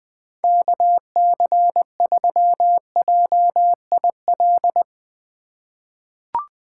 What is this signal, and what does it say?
Morse code, heard as a keyed tone, about 21 wpm, 700 Hz. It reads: KC3JIL
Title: USA Call Signs - 6 Characters - 2x Character Spacing - 20wpm